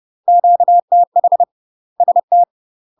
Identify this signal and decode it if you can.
Morse code, heard as a keyed tone, about 30 wpm, 700 Hz. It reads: QTH ST
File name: UK MASTER QSOs TRAINING / 30WPM / UK MASTER QSOs 0047